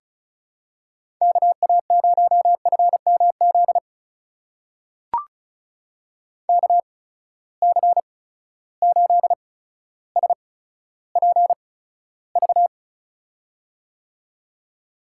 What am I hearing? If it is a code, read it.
Morse code, heard as a keyed tone, about 36 wpm, spaced as for 15 wpm, 700 Hz. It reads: KA0FMZ KC8SPV
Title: USA Call Signs - 6 Characters - 8x Character Spacing - 35wpm